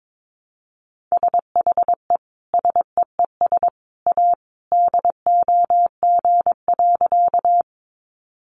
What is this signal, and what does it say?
Morse code, heard as a keyed tone, about 22 wpm, 700 Hz. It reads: SHE SEES A DOG.
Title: Easy Phrases 22wpm